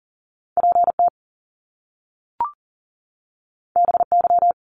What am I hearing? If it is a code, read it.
Morse code, heard as a keyed tone, about 38 wpm, 700 Hz. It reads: PT BY
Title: Top Two Letter Combinations in Words 40wpm